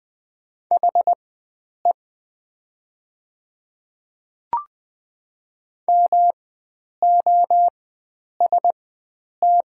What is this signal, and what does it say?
Morse code, heard as a keyed tone, about 20 wpm, spaced as for 9 wpm, 700 Hz. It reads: HE MOST
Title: 4x Character Spacing - Top 100 Words - 20wpm